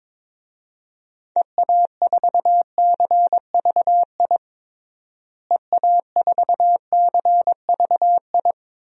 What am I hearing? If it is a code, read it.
Morse code, heard as a keyed tone, about 22 wpm, 700 Hz. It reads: EA4CVI EA4CVI
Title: DXCC-100 Call Signs - 6 Characters - Repeated 2x - 22wpm